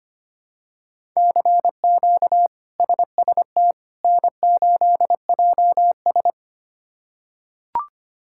Morse code, heard as a keyed tone, about 25 wpm, 700 Hz. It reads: CQ SST N8JS
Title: K1USN SST QSO Practice - 25wpm